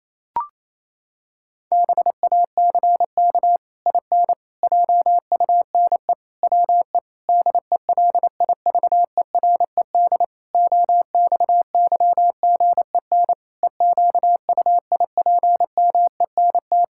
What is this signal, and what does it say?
Morse code, heard as a keyed tone, about 28 wpm, 700 Hz. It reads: BACK IN JUNE WE DELIVERED OXYGEN EQUIPMENT
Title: Pangram Challenge v2 - 28wpm